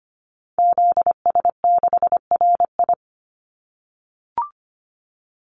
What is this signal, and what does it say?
Morse code, heard as a keyed tone, about 25 wpm, 700 Hz. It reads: ZS6RI
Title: Call Signs - 25wpm